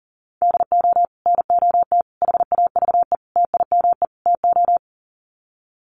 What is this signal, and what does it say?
Morse code, heard as a keyed tone, about 40 wpm, 700 Hz. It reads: DO NOT HAVE TIME TO
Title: Sets of 5 Words 40wpm